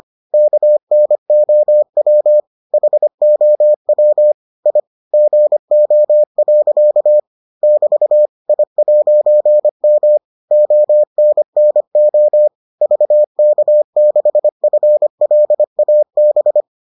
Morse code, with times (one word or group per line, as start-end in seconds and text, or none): 0.3-2.4 KNOW
2.7-4.3 HOW
4.7-4.8 I
5.1-7.2 GO.
7.6-8.3 <BT>
8.5-10.2 I'M
10.5-12.5 ONNO
12.8-16.6 VK6FLAB